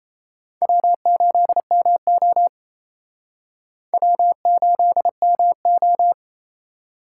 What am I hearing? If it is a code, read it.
Morse code, heard as a keyed tone, about 36 wpm, 700 Hz. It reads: W8MO W8MO